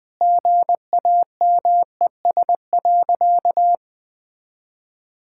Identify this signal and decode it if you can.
Morse code, heard as a keyed tone, about 20 wpm, 700 Hz. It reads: GAMES.
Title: Easy Sentences with Hints - 20wpm